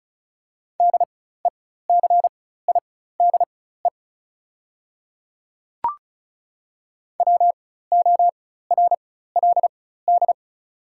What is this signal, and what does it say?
Morse code, heard as a keyed tone, about 36 wpm, spaced as for 16 wpm, 700 Hz. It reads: DECIDE WORLD
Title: 4x Character Spacing - Top 500 Words - 35wpm